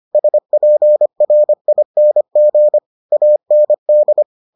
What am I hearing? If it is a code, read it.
Morse code, heard as a keyed tone, about 25 wpm, 600 Hz. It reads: SPRING AND